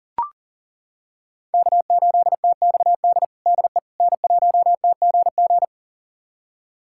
Morse code, heard as a keyed tone, about 40 wpm, 700 Hz. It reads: K8TXD DE N1TGG
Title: USA Callsign DE Callsign - 5 Characters - 40wpm